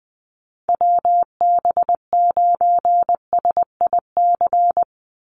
Morse code, heard as a keyed tone, about 20 wpm, 700 Hz. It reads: WB9SIC
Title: USA Call Signs 20wpm